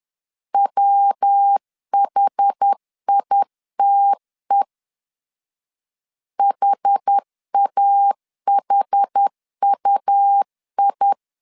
Morse code, heard as a keyed tone, about 10 wpm, 800 Hz. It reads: WHITE HAHUI
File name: CW / DET / 1 3 1500C S A 10wpm 3447